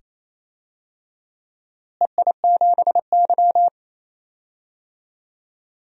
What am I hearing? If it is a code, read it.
Morse code, heard as a keyed tone, about 28 wpm, 700 Hz. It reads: EI7Y
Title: DXCC-100 Call Signs - 4 Characters - Repeated 2x - 28wpm